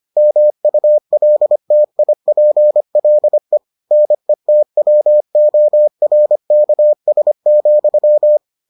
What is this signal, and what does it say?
Morse code, heard as a keyed tone, about 25 wpm, 600 Hz. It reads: MULTIPLE NETWORKS,